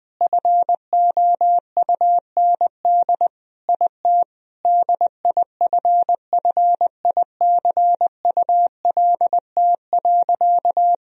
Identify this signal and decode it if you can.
Morse code, heard as a keyed tone, about 20 wpm, 700 Hz. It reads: FOUND IT DIFFICULT.
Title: Sentences from Top 1000 Words - Review Part 3 - 20wpm